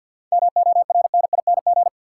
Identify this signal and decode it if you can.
Morse code, heard as a keyed tone, about 52 wpm, 700 Hz. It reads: MORNING
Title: Sentences from Top 400 Words - New Word Drill - 50wpm